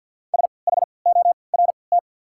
Morse code, heard as a keyed tone, about 48 wpm, spaced as for 28 wpm, 700 Hz. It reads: SHORT
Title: 3x Character Spacing - Top 500 Words - 50wpm